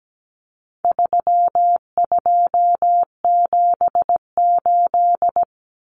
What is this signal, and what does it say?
Morse code, heard as a keyed tone, about 17 wpm, 700 Hz. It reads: 3278
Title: Common Cut Numbers - 4-Digit 17wpm